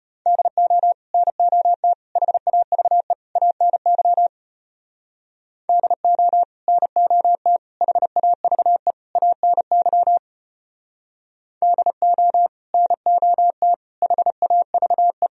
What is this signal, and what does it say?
Morse code, heard as a keyed tone, about 33 wpm, 700 Hz. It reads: DO NOT HAVE ANY DO NOT HAVE ANY DO NOT HAVE